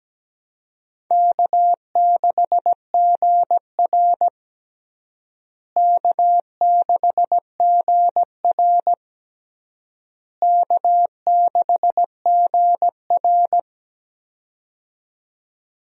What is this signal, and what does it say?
Morse code, heard as a keyed tone, about 17 wpm, 700 Hz. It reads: K6GR K6GR K6GR
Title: USA Call Signs - 4 Characters - Repeated 3x - 17wpm